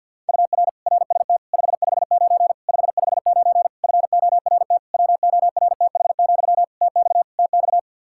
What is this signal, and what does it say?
Morse code, heard as a keyed tone, about 52 wpm, 700 Hz. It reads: UR RST 559 559 FORT WORTH, TX TX
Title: Practice QSO - Exchange 1 - Beginning - 50wpm